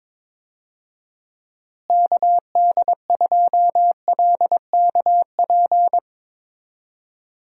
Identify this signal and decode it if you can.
Morse code, heard as a keyed tone, about 21 wpm, 700 Hz. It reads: KD2LKP
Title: USA Call Signs - 6 Characters - Repeated 3x - 22wpm